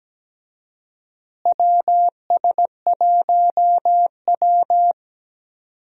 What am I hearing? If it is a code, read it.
Morse code, heard as a keyed tone, about 17 wpm, 700 Hz. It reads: WS1W
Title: USA Call Signs - 4 Characters - Repeated 2x - 17wpm